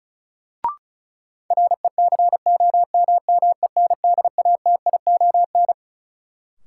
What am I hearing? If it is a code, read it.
Morse code, heard as a keyed tone, about 36 wpm, 700 Hz. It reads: RECOMMENDATION
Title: Sentences from Top 1400 Words - New Word Drill - 35wpm